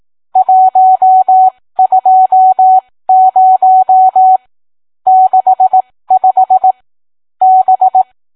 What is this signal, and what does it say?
Morse code, heard as a keyed tone, about 18 wpm, spaced as for 15 wpm, 750 Hz. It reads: 120 65 B